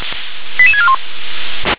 Five-tone selective call (ZVEI), heard as quick runs of five tones, tones of about 80 ms. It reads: 88641
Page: /357/SDR/ZVEI/